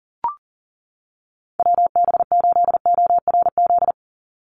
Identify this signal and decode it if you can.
Morse code, heard as a keyed tone, about 40 wpm, 700 Hz. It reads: WB8ORZ
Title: USA Call Signs 40wpm